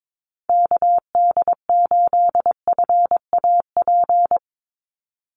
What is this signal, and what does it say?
Morse code, heard as a keyed tone, about 22 wpm, 700 Hz. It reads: KD8FAP
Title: USA Call Signs 22wpm